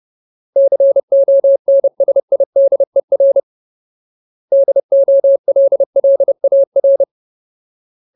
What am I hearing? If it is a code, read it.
Morse code, heard as a keyed tone, about 30 wpm, 550 Hz. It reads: CONSIDER DOLLAR
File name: Ultimate Word List / 30WPM Ultimate